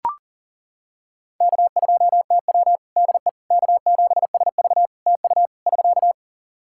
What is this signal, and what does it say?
Morse code, heard as a keyed tone, about 40 wpm, 700 Hz. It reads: K2TW DE K7SV TU <SK>
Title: K1USN SST QSO Practice 40wpm